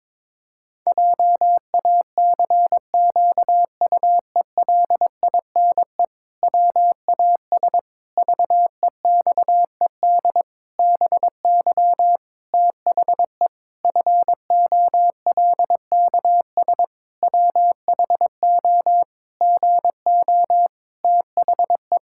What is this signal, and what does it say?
Morse code, heard as a keyed tone, about 22 wpm, 700 Hz. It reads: JACQUELINE WAS VEXED BY THE FOLKS WHO GO THE
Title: Pangram Challenge v2 - 22wpm